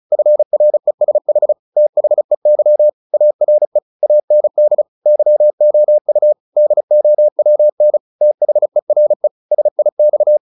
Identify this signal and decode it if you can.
Morse code, heard as a keyed tone, about 34 wpm, 600 Hz. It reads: FRESH THEY ARE AND YOU DOWN THERE SIX
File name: the old man and the sea / the old man and the sea 1 35wpm